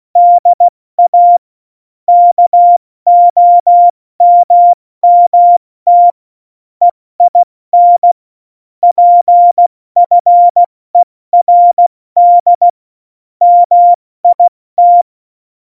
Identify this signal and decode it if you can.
Morse code, heard as a keyed tone, about 16 wpm, 700 Hz. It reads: DA KOMMT EIN PFERD MIT
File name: Morsekurs / Witze / Witz 19. Juni 2018, 70 F80